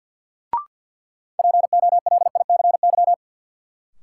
Morse code, heard as a keyed tone, about 53 wpm, 700 Hz. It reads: POLICY
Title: Sentences from Top 1000 Words - Review Part 5 - New Word Drill - 50wpm